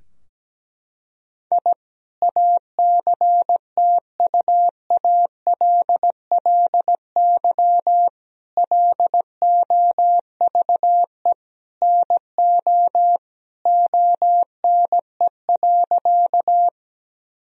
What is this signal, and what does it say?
Morse code, heard as a keyed tone, about 17 wpm, 700 Hz. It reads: I ACTUALLY LOVE NO ONE.
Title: Sentences from Top 1000 Words - Review Part 4 - 17wpm